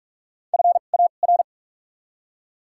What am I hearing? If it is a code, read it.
Morse code, heard as a keyed tone, about 42 wpm, spaced as for 31 wpm, 700 Hz. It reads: FAR
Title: 2x Character Spacing - Top 500 Words - 45wpm